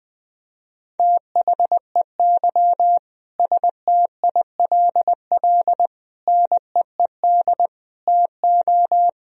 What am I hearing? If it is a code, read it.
Morse code, heard as a keyed tone, about 21 wpm, 700 Hz. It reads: THEY STILL NEED TO
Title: Sentences from Top 1000 Words - Review Part 4 - 20wpm